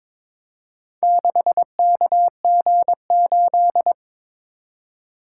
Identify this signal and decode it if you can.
Morse code, heard as a keyed tone, about 22 wpm, 700 Hz. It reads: 6KG8
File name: practice / 22wpm / Four Letter-Number Combinations - 22wpm